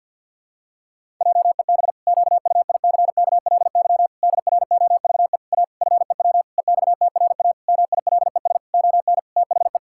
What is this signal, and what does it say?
Morse code, heard as a keyed tone, about 49 wpm, 700 Hz. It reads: JEB QUICKLY DROVE A FEW EXTRA MILES ON THE
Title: Pangram Challenge v2 - 50wpm